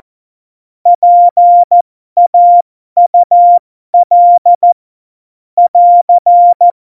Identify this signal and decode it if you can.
Morse code, heard as a keyed tone, about 14 wpm, spaced as for 12 wpm, 700 Hz. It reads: PAUL <AR>
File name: Morsekurs / Witze